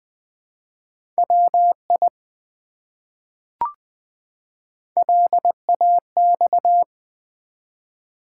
Morse code, heard as a keyed tone, about 20 wpm, 700 Hz. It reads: WI LAX
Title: ARRL Sections v2 - 20wpm